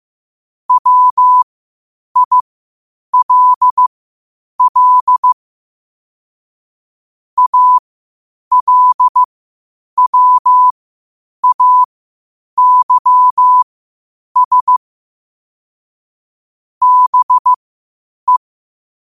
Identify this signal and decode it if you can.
Morse code, heard as a keyed tone, about 15 wpm, spaced as for 8 wpm, 1000 Hz. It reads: WILL ALWAYS BE
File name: qotdfiles / 8wpm